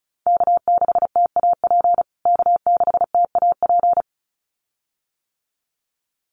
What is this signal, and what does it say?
Morse code, heard as a keyed tone, about 34 wpm, 700 Hz. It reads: K6TAP K6TAP
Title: USA Call Signs Repeated 2x 35wpm